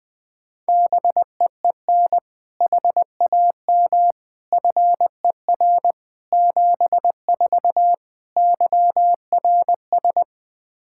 Morse code, heard as a keyed tone, about 20 wpm, 700 Hz. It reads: BEEN HAM FER 74 YRS
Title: Practice QSO - Exchange 3 - More Info - 20wpm